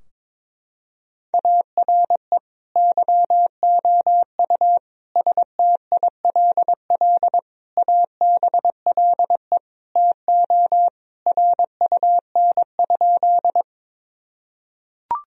Morse code, heard as a keyed tone, about 21 wpm, 700 Hz. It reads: ARE YOU STILL ABLE TO RUN?
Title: Sentences from Top 1000 Words - Review Part 2 - 22wpm